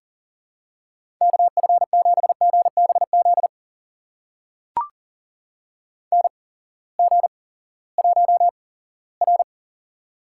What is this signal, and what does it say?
Morse code, heard as a keyed tone, about 40 wpm, spaced as for 20 wpm, 700 Hz. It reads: KF7GBZ NG1R
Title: USA Call Signs - 8x Character Spacing - 40wpm